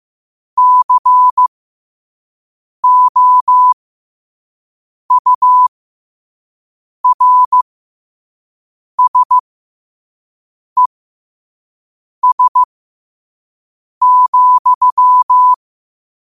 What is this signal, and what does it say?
Morse code, heard as a keyed tone, about 15 wpm, spaced as for 5 wpm, 1000 Hz. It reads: COURSES,